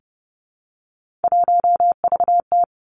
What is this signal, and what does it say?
Morse code, heard as a keyed tone, about 30 wpm, 700 Hz. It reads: 1VT
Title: Three Challenging Letter-Number Combinations 30wpm